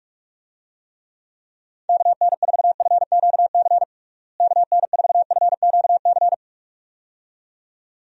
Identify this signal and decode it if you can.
Morse code, heard as a keyed tone, about 46 wpm, 700 Hz. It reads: KN4FQC KN4FQC
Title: USA Call Signs - 6 Characters - Repeated 2x - 45wpm